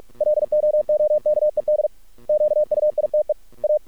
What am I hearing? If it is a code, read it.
Morse code, heard as a keyed tone, about 48 wpm, 600 Hz. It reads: LOOKED QUITE A
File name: archived / CRSnet